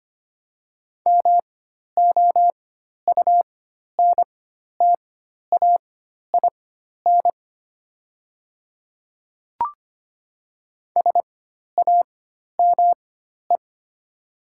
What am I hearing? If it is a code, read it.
Morse code, heard as a keyed tone, about 25 wpm, spaced as for 12 wpm, 700 Hz. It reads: MOUNTAIN SAME
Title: 4x Character Spacing - Top 500 Words - 25wpm